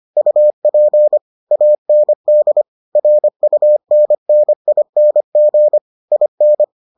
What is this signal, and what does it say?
Morse code, heard as a keyed tone, about 25 wpm, 600 Hz. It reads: UP AND RUNNING IN